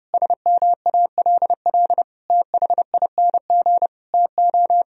Morse code, heard as a keyed tone, about 32 wpm, 700 Hz. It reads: SMALL THING TO
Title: Sentences from Top 1000 Words - Review Part 1 - 30wpm